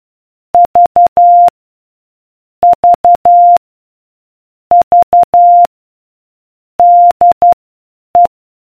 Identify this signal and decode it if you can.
Morse code, heard as a keyed tone, about 12 wpm, spaced as for 8 wpm, 700 Hz. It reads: V V V DE